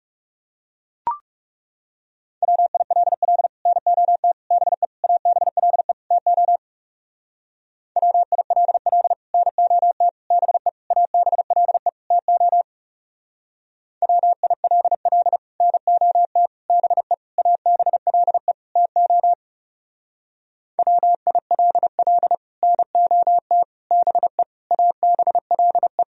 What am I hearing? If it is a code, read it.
Morse code, heard as a keyed tone, about 40 wpm, 700 Hz. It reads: WILL NOT BE ABLE TO WILL NOT BE ABLE TO WILL NOT BE ABLE TO WILL NOT BE ABLE